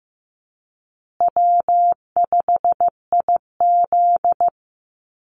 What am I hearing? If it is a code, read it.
Morse code, heard as a keyed tone, about 15 wpm, 700 Hz. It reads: W5IZ
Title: Easy Call Signs Repeated 3x - 15wpm